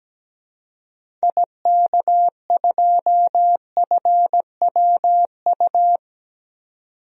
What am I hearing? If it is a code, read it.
Morse code, heard as a keyed tone, about 17 wpm, 700 Hz. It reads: IK2FWU
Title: DXCC-100 Call Signs - 6 Characters - 17wpm